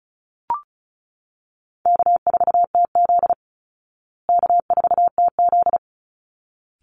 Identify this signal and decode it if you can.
Morse code, heard as a keyed tone, about 34 wpm, 700 Hz. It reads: K4TZ K4TZ
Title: Call Signs Repeated 3x - 35wpm